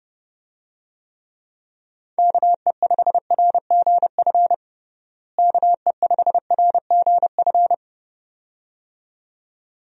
Morse code, heard as a keyed tone, about 30 wpm, 700 Hz. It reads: KE5RGF KE5RGF
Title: USA Call Signs - 6 Characters - Repeated 2x - 30wpm